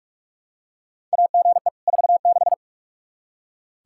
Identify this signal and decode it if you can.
Morse code, heard as a keyed tone, about 46 wpm, 700 Hz. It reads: AGE 46
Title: K1USN SST QSO Practice 45wpm